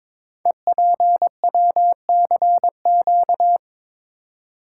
Morse code, heard as a keyed tone, about 23 wpm, 700 Hz. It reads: EPWCQ
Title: Code Groups - 5 Letter:Numbers - 22wpm